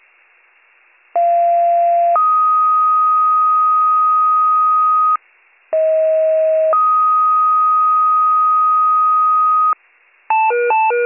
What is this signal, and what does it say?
Two-tone page, heard as a one-second tone, then a three-second tone, about 690 then 1250 Hz, about 640 then 1250 Hz. It is heard from 1.2 to 9.7 s.